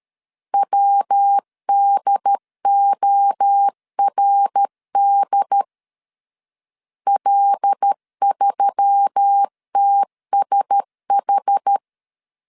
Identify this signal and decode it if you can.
Morse code, heard as a keyed tone, about 13 wpm, 800 Hz. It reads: WDORD L3TSH
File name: CW / DET / 1 6 1500C S A 12wpm 3440